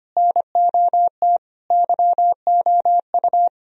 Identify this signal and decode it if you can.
Morse code, heard as a keyed tone, about 31 wpm, 700 Hz. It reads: NOT YOU
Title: Speed-Racing - Sets of 5 Words 25wpm